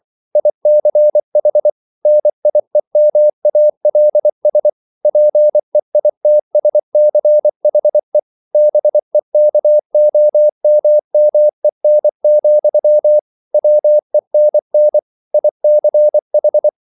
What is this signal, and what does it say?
Morse code, heard as a keyed tone, about 24 wpm, 600 Hz. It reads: ICH NIEMALS PEITSCHE BEKOMMEN, WENN ICH